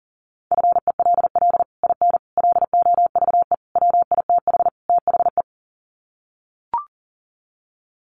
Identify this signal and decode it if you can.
Morse code, heard as a keyed tone, about 40 wpm, 700 Hz. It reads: FELL IN LOVE WITH THE
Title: Sets of 5 Words Encore2 40wpm